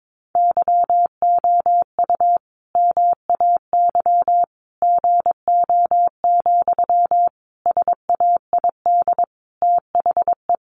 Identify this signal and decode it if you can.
Morse code, heard as a keyed tone, about 22 wpm, 700 Hz. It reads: YOU MAY GO, SAID THE
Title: Sentences from Top 400 Words 22wpm